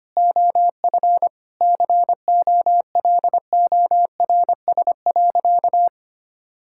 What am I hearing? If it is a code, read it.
Morse code, heard as a keyed tone, about 25 wpm, 700 Hz. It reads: OF COLORS.
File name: practice / 25wpm / Sentences with 4-Word Phrases - 25wpm